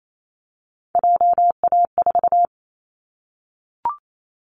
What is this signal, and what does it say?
Morse code, heard as a keyed tone, about 28 wpm, 700 Hz. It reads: JA4
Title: Callsign Prefixes for Top 25 DXCC - Repeated 3x - 28wpm